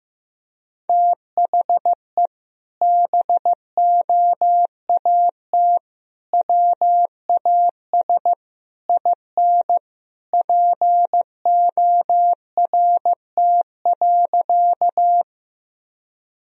Morse code, heard as a keyed tone, about 15 wpm, 700 Hz. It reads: THE BOAT WAS IN PORT.
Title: Sentences from Top 1000 Words - Review Part 1 - 15wpm